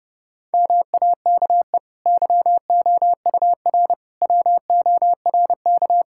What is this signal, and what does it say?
Morse code, heard as a keyed tone, about 30 wpm, 700 Hz. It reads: MAKE YOUR WORK
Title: Sentences from Top 1000 Words - Review Part 6 - 30wpm